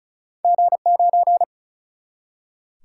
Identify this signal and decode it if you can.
Morse code, heard as a keyed tone, about 36 wpm, 700 Hz. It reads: G9
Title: Single-Letter Single-Number 35wpm